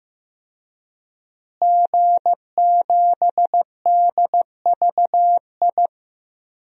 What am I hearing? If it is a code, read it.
Morse code, heard as a keyed tone, about 15 wpm, 700 Hz. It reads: G7DVI
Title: DXCC-100 Call Signs - 5 Characters - Repeated 3x - 15wpm